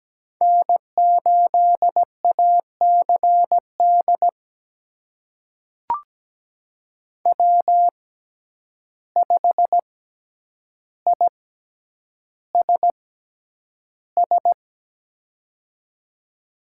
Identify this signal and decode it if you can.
Morse code, heard as a keyed tone, about 17 wpm, spaced as for 9 wpm, 700 Hz. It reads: N8ACD W5ISS
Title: USA Call Signs - 5 Characters - 6x Character Spacing - 17wpm